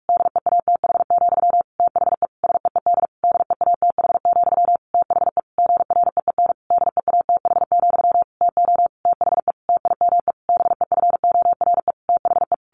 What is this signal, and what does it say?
Morse code, heard as a keyed tone, about 44 wpm, 700 Hz. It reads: DEATH, THE SEED DEATH, THE GREEN DEATH, TO THE TIME BEFORE THE
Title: A Sound Of Thunder 45wpm